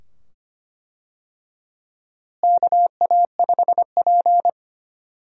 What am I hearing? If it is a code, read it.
Morse code, heard as a keyed tone, about 25 wpm, 700 Hz. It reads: KA5P
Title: USA Call Signs - 4 Characters - Repeated 2x - 25wpm